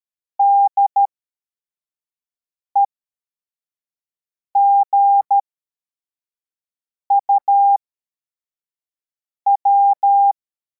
Morse code, heard as a keyed tone, about 13 wpm, spaced as for 4 wpm, 800 Hz. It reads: DEGUW
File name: formation / cw / Text61